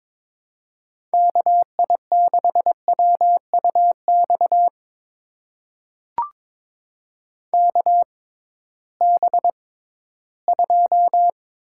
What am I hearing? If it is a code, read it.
Morse code, heard as a keyed tone, about 21 wpm, spaced as for 14 wpm, 700 Hz. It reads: KI6WUX KB2
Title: USA Call Signs - 6 Characters - 6x Character Spacing - 22wpm